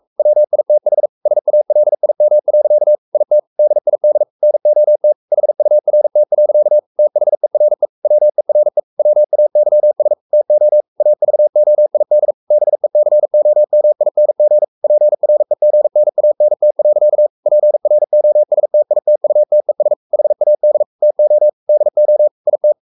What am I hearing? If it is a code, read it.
Morse code, heard as a keyed tone, about 44 wpm, 600 Hz. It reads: WITH SALIM. IT DID NOT HURT. THERE WERE WAYS TO AVOID BECOMING PREGNANT. PROSTITUTES HAD TO DO IT
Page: Romanen Never af Ken Follet på engelsk.